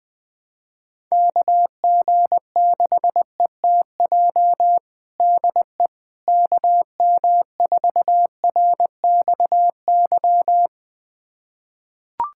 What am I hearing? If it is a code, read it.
Morse code, heard as a keyed tone, about 21 wpm, 700 Hz. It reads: KG6ETJ DE KM4RXY
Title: USA Callsign DE Callsign 20wpm